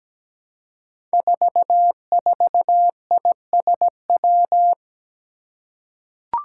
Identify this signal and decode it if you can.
Morse code, heard as a keyed tone, about 17 wpm, 700 Hz. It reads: 44ISW
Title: Five Challenging Letter-Number Combinations 17wpm